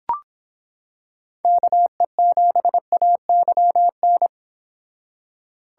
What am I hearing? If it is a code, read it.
Morse code, heard as a keyed tone, about 25 wpm, 700 Hz. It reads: KE7AYN